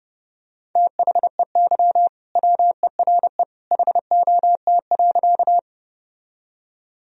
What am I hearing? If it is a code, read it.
Morse code, heard as a keyed tone, about 30 wpm, 700 Hz. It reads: THEY WERE HOT.
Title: Sentences from Top 1000 Words - Review Part 3 - 30wpm